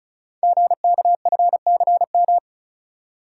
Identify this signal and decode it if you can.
Morse code, heard as a keyed tone, about 35 wpm, 700 Hz. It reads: GKFCM
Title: Code Groups - 5 Letters - 35wpm